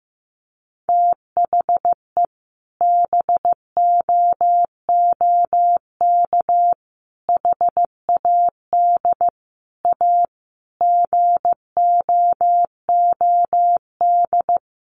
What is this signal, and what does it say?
Morse code, heard as a keyed tone, about 15 wpm, 700 Hz. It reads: THE BOOK HAD A GOOD
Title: Sentences from Top 1400 Words 15wpm